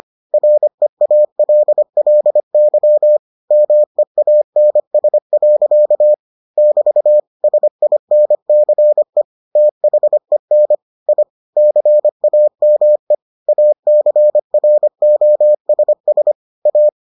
Morse code, heard as a keyed tone, about 25 wpm, 600 Hz. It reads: REALLY MEANS. <BT> SINCE THEN I CAME ACROSS A